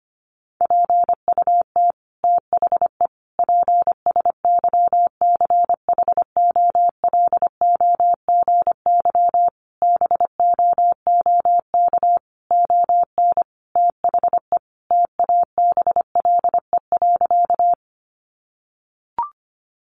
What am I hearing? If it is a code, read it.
Morse code, heard as a keyed tone, about 25 wpm, 700 Hz. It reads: PUT THE PSYCHOLOGY BOOK ON THE TABLE.
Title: Sentences from Top 1400 Words 25wpm